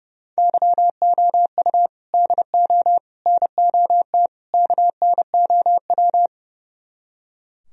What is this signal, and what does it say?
Morse code, heard as a keyed tone, about 38 wpm, 700 Hz. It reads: YOU DO NOT KNOW